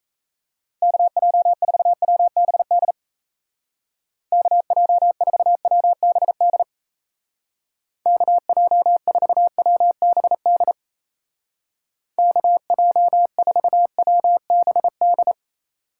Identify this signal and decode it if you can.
Morse code, heard as a keyed tone, about 34 wpm, 700 Hz. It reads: KJ4WBD KJ4WBD KJ4WBD KJ4WBD
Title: Speed Racing - USA 2x3 Call Signs 28wpm